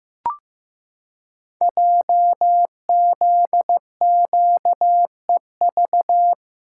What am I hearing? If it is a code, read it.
Morse code, heard as a keyed tone, about 15 wpm, 700 Hz. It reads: JZQEV
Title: Code Groups - 5 Letters - 15wpm